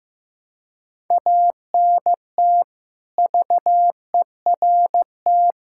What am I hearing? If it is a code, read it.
Morse code, heard as a keyed tone, about 15 wpm, 700 Hz. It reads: ANT VERT
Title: K1USN SST QSO Practice 15wpm